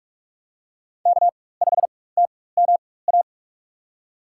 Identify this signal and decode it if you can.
Morse code, heard as a keyed tone, about 46 wpm, spaced as for 21 wpm, 700 Hz. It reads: K5TMA
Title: USA Call Signs - 5 Characters - 4x Character Spacing - 45wpm